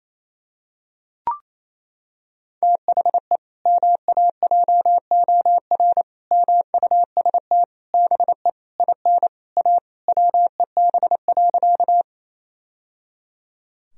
Morse code, heard as a keyed tone, about 28 wpm, 700 Hz. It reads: THE MAJOR MUST BE IN A WEB.
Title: Sentences from Top 1000 Words - Review Part 4 - 28wpm